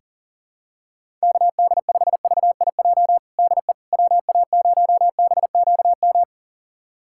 Morse code, heard as a keyed tone, about 40 wpm, 700 Hz. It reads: KD5VIJ DE WA0BQM
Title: USA Callsign DE Callsign - 6 Characters - 40wpm